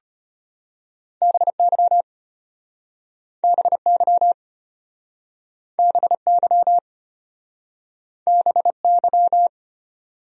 Morse code, heard as a keyed tone, about 33 wpm, 700 Hz. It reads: BY BY BY BY